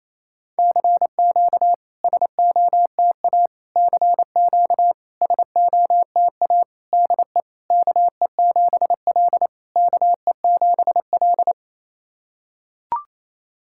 Morse code, heard as a keyed tone, about 28 wpm, 690 Hz. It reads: CQ SOTA CQ SOTA DE KE7L KE7L
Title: SOTA QSO Practice 28wpm